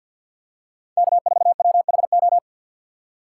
Morse code, heard as a keyed tone, about 53 wpm, 700 Hz. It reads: K4WHO
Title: USA Call Signs - 5 Characters - 4x Character Spacing - 50wpm